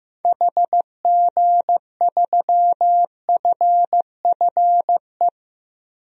Morse code, heard as a keyed tone, about 15 wpm, 700 Hz. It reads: HG3FFE
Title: DXCC-100 Call Signs - 6 Characters - 15wpm